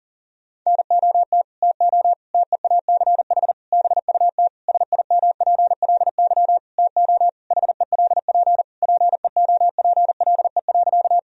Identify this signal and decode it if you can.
Morse code, heard as a keyed tone, about 40 wpm, 700 Hz. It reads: NOT TO TEACH BUT SIMPLY TO HELP PEOPLE.
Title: Sentences from Top 1000 Words - Review Part 4 - 40wpm